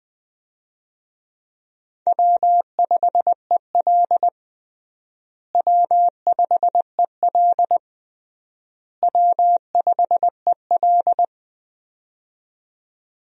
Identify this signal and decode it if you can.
Morse code, heard as a keyed tone, about 20 wpm, 700 Hz. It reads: W5EL W5EL W5EL
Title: USA Call Signs - 4 Characters - Repeated 3x - 20wpm